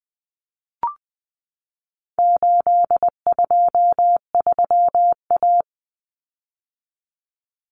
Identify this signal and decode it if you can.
Morse code, heard as a keyed tone, about 20 wpm, 700 Hz. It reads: 823A
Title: Common Cut Numbers - 4-Digit 20wpm